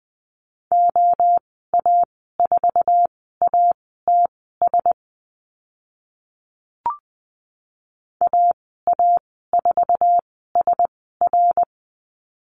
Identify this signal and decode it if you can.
Morse code, heard as a keyed tone, about 20 wpm, spaced as for 14 wpm, 700 Hz. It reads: OA4ATS AA4SR
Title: 062 Lesson 11g - Letter h - Call Signs 20wpm